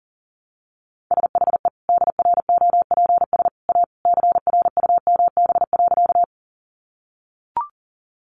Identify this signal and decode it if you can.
Morse code, heard as a keyed tone, about 40 wpm, 700 Hz. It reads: SHE DROPS A CRUMB.
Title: Easy Phrases 40wpm